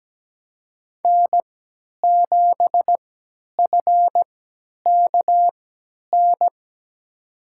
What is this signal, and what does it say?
Morse code, heard as a keyed tone, about 17 wpm, spaced as for 10 wpm, 700 Hz. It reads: N7FKN
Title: USA Call Signs - 5 Characters - 3x Character Spacing - 17wpm